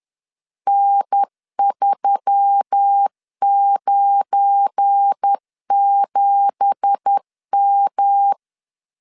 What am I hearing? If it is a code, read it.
Morse code, heard as a keyed tone, about 10 wpm, 800 Hz. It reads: N397M
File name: LEZIONI / 9 2867 10wpm 2733